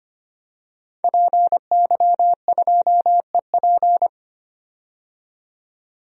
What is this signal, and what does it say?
Morse code, heard as a keyed tone, about 25 wpm, 700 Hz. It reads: PY2EP
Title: DXCC-100 Call Signs - 5 Characters - 25wpm